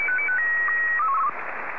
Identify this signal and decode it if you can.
SSTV (slow-scan television), Martin 1.
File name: sstv